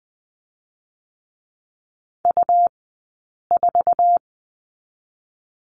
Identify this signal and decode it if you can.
Morse code, heard as a keyed tone, about 20 wpm, spaced as for 8 wpm, 700 Hz. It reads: U4
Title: 126 Lesson 21d - Number 6 - Two Characters 20wpm